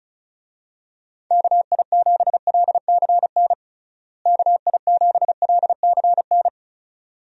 Morse code, heard as a keyed tone, about 36 wpm, 700 Hz. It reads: KI7LCN KI7LCN
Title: USA Call Signs - 6 Characters - Repeated 2x - 35wpm